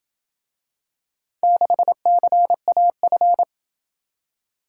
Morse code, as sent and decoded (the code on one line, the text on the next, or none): -.... -.-. .- ..-.
6CAF